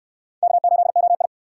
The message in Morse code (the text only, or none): K8OI